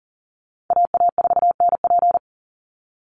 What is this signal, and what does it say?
Morse code, heard as a keyed tone, about 40 wpm, 700 Hz. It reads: AA4NP
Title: Call Signs Repeated 2x - 40wpm